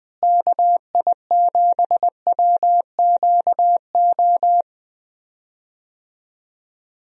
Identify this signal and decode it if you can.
Morse code, heard as a keyed tone, about 20 wpm, 700 Hz. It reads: KI7WQO
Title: USA Call Signs - 6 Characters - Repeated 3x - 20wpm